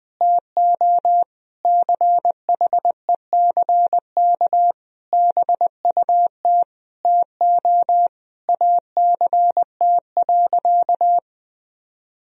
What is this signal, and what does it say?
Morse code, heard as a keyed tone, about 20 wpm, 700 Hz. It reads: TO CHECK BUT TO ACT.
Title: Sentences from Top 1000 Words - Review Part 3 - 20wpm